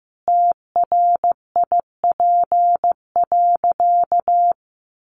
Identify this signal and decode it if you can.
Morse code, heard as a keyed tone, about 14 wpm, 700 Hz. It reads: TRIP.
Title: Sentences from Top 1500 Words 15wpm